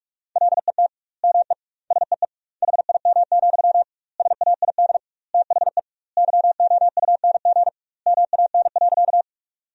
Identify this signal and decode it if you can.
Morse code, heard as a keyed tone, about 45 wpm, 700 Hz. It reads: LET ME SEE HIM, SAID THE YOUNG MAN.
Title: 2x Word-Spacing - Sentences from Top 400 Words - 45wpm